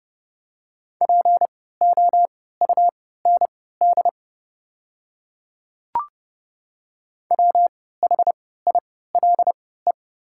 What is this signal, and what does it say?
Morse code, heard as a keyed tone, about 32 wpm, spaced as for 17 wpm, 700 Hz. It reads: POUND WHILE
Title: 3x Character Spacing - Top 500 Words - 30wpm